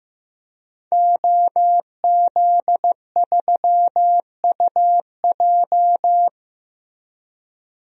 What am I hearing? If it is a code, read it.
Morse code, heard as a keyed tone, about 15 wpm, 700 Hz. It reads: OZ3UJ
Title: DXCC-100 Call Signs - 5 Characters - Repeated 2x - 15wpm